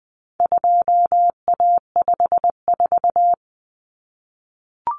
Morse code, heard as a keyed tone, about 20 wpm, 700 Hz. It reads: 2A54
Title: Common Cut Numbers - 4-Digit 20wpm